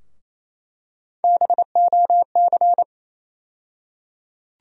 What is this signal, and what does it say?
Morse code, heard as a keyed tone, about 27 wpm, 700 Hz. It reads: BOC
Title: USA Call Sign Postfixes - 3 Letters 28wpm